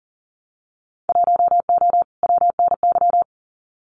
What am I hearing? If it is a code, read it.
Morse code, heard as a keyed tone, about 40 wpm, 700 Hz. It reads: 1O WNY
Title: Winter Field Day Exchanges Encore 40wpm